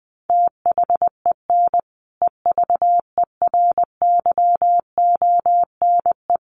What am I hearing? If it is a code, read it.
Morse code, heard as a keyed tone, about 20 wpm, 700 Hz. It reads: THEN EVERYONE
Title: Sentences from Top 1400 Words 20wpm